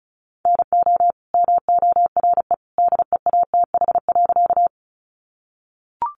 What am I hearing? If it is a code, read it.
Morse code, heard as a keyed tone, about 34 wpm, 700 Hz. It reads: NO MORE DEATH.
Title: Sentences from Top 1100 Words 35wpm